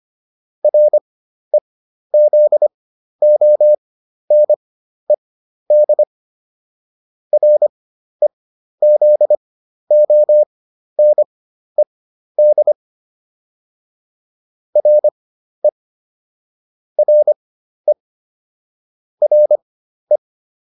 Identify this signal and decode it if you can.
Morse code, heard as a keyed tone, about 25 wpm, spaced as for 12 wpm, 600 Hz. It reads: REZONED REZONED RE RE RE